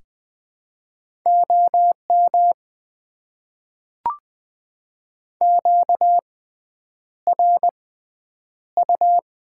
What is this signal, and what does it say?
Morse code, heard as a keyed tone, about 20 wpm, spaced as for 9 wpm, 700 Hz. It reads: OM QRU
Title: Top 90 QSO Elements - 6x Character Spacing - 20wpm